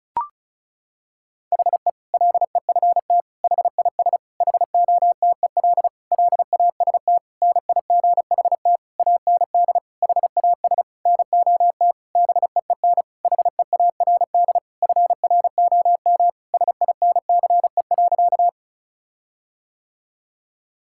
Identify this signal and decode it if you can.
Morse code, heard as a keyed tone, about 36 wpm, 700 Hz. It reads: HE LEFT HIS HOTEL LAST NIGHT AND HAS NOT BEEN HEARD FROM SINCE.
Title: Sentences from Top 1600 Words 35wpm